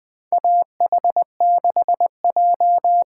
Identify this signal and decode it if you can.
Morse code, heard as a keyed tone, about 20 wpm, 700 Hz. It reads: AH6J
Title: USA Call Signs - 5 Characters - 8x Character Spacing - 20wpm